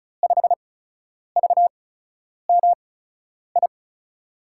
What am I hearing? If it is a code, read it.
Morse code, heard as a keyed tone, about 34 wpm, spaced as for 10 wpm, 700 Hz. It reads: 5VMI
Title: USA Call Signs - 6 Characters - 8x Character Spacing - 35wpm